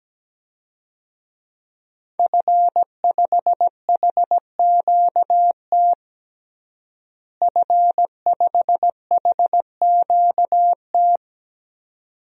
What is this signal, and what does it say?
Morse code, heard as a keyed tone, about 17 wpm, 700 Hz. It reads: F5HQT F5HQT
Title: DXCC-100 Call Signs - 5 Characters - Repeated 2x - 17wpm